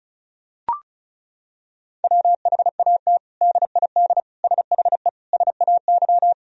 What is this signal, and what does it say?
Morse code, heard as a keyed tone, about 34 wpm, 700 Hz. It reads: WHAT DID SHE SAY
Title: Sentences from Top 1000 Words - Review Part 2 - 35wpm